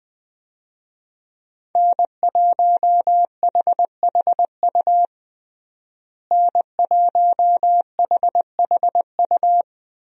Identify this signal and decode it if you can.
Morse code, heard as a keyed tone, about 20 wpm, 700 Hz. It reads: N1HHU N1HHU
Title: USA Call Signs - 5 Characters - Repeated 3x - 20wpm